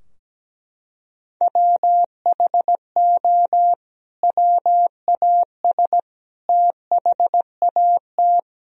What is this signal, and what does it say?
Morse code, heard as a keyed tone, about 17 wpm, 700 Hz. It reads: WHO WAS THAT
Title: Sentences from Top 1000 Words - Review Part 4 - 17wpm